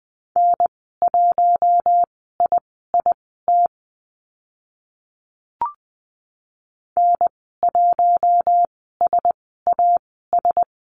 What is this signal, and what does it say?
Morse code, heard as a keyed tone, about 20 wpm, spaced as for 15 wpm, 700 Hz. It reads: N1IIT N1SAS
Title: 042 Lesson 8g - Number 1 - Call Signs 20wpm